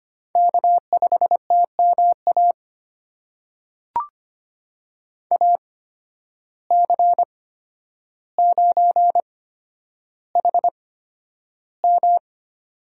Morse code, heard as a keyed tone, about 25 wpm, spaced as for 11 wpm, 700 Hz. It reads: K5TMA AC9HM
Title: USA Call Signs - 5 Characters - 8x Character Spacing - 25wpm